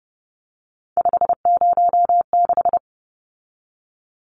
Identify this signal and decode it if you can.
Morse code, heard as a keyed tone, about 30 wpm, 700 Hz. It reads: 506
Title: Numbers - 3-Digit 30wpm